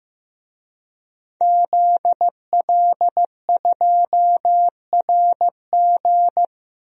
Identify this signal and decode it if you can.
Morse code, heard as a keyed tone, about 16 wpm, 700 Hz. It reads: ZL2RG
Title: DXCC-100 Call Signs - 5 Characters - Repeated 3x - 15wpm